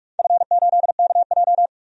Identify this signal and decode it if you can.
Morse code, heard as a keyed tone, about 44 wpm, 690 Hz. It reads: F8KJ
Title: DXCC-100 Call Signs - 4 Characters - 45wpm